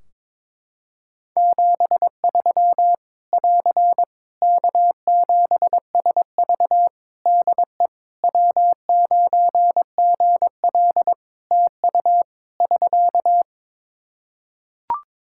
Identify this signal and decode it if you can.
Morse code, heard as a keyed tone, about 22 wpm, 700 Hz. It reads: 73 <AR> K7SV DE W9GL TU <SK>
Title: Practice QSO - Exchange 4 - Ending - 22wpm